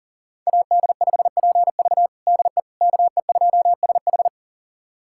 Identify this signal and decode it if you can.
Morse code, heard as a keyed tone, about 40 wpm, 700 Hz. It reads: AD5PV DE KE2SH
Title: USA Callsign DE Callsign - 5 Characters - 40wpm